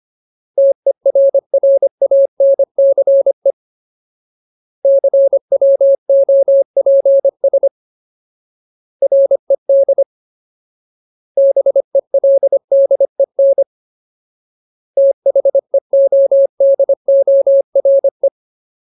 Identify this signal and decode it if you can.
Morse code, heard as a keyed tone, about 25 wpm, 550 Hz. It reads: TERRANCE CWOPS RED BELDEN THEODORE
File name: Ultimate Word List / 25WPM Ultimate